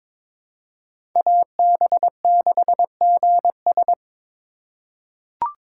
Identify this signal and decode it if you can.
Morse code, heard as a keyed tone, about 22 wpm, 700 Hz. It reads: AB6GS
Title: USA Call Signs - 5 Characters - 3x Character Spacing - 22wpm